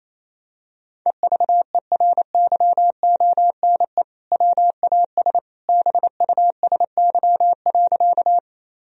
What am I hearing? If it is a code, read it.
Morse code, heard as a keyed tone, about 27 wpm, 700 Hz. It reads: EVERYONE WAS BUSY.
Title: Sentences from Top 1000 Words - Review Part 3 - 28wpm